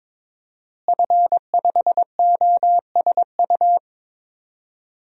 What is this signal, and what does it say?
Morse code, heard as a keyed tone, about 22 wpm, 700 Hz. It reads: F5OSU
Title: DXCC-100 Call Signs - 5 Characters - Repeated 2x - 22wpm